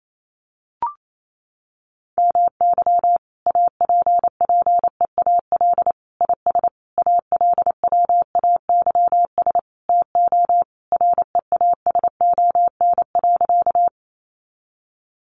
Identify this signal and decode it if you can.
Morse code, heard as a keyed tone, about 28 wpm, 700 Hz. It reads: MY APPEAL IS ALWAYS TO REASON.
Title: Sentences from Top 1300 Words 28wpm